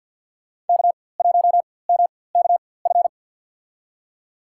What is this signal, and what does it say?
Morse code, heard as a keyed tone, about 52 wpm, spaced as for 24 wpm, 700 Hz. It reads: K1MKF
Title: USA Call Signs - 4x Character Spacing - 50wpm